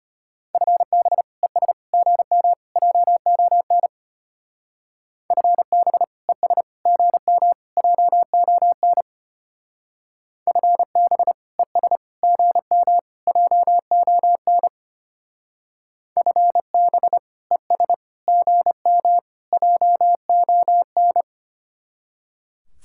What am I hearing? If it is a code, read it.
Morse code, heard as a keyed tone, about 31 wpm, 700 Hz. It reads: FB ES GM JON FB ES GM JON FB ES GM JON FB ES GM JON
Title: Speed-Racing Common QSO Phrases Encore - 25wpm